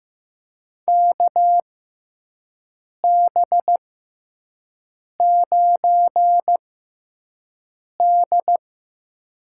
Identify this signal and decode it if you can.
Morse code, heard as a keyed tone, about 15 wpm, spaced as for 5 wpm, 700 Hz. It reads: KB9D
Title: USA Call Signs - 4 Characters - 6x Character Spacing - 15wpm